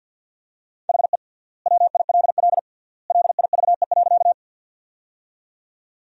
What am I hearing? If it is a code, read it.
Morse code, heard as a keyed tone, about 51 wpm, 700 Hz. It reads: HE WILL LIVE.
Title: 3x Word-Spacing - Sentences from Top 200 Words - 50wpm